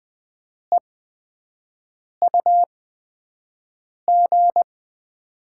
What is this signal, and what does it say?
Morse code, heard as a keyed tone, about 20 wpm, spaced as for 5 wpm, 700 Hz. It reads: EUG
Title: USA Call Signs - 5 Characters - 8x Character Spacing - 20wpm